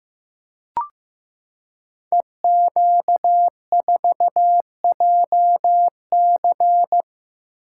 Morse code, heard as a keyed tone, about 15 wpm, 700 Hz. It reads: EQ4JC
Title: Code Groups - 5 Letter:Numbers - 15wpm